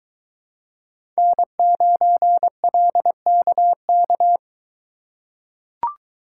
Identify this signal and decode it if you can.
Morse code, heard as a keyed tone, about 21 wpm, 700 Hz. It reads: N9LKK